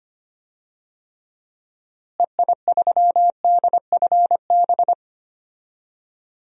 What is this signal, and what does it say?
Morse code, heard as a keyed tone, about 25 wpm, 700 Hz. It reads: EI3DFB
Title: DXCC-100 Call Signs - 6 Characters - Repeated 3x - 25wpm